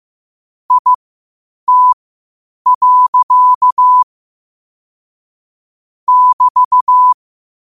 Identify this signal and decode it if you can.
Morse code, heard as a keyed tone, about 15 wpm, spaced as for 8 wpm, 1000 Hz. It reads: IT. <BT>
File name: qotdfiles / 8wpm